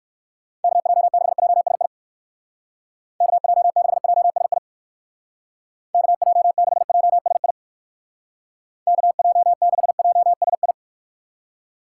Morse code, heard as a keyed tone, about 59 wpm, 700 Hz. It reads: KJ6JSI KJ6JSI KJ6JSI KJ6JSI